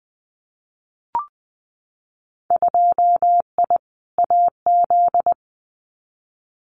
Morse code, heard as a keyed tone, about 20 wpm, 700 Hz. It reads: 2I AZ
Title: Winter Field Day Exchanges Encore 20wpm